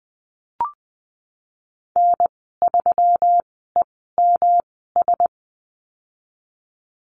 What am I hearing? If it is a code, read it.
Morse code, heard as a keyed tone, about 20 wpm, spaced as for 15 wpm, 700 Hz. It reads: N3EMS
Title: 122 Lesson 20g - Number 3 - Call Signs 20wpm